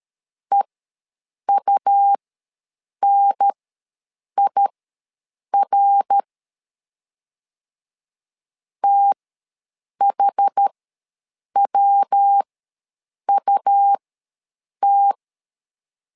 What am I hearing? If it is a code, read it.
Morse code, heard as a keyed tone, about 13 wpm, spaced as for 7 wpm, 800 Hz. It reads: EUNIR THWUT
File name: LEZIONI / 3 MWRO / 3 MWR ws 12wpm 1809